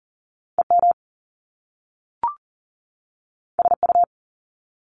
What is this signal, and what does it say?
Morse code, heard as a keyed tone, about 40 wpm, 700 Hz. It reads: EM SU
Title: Top Two Letter Combinations in Words 40wpm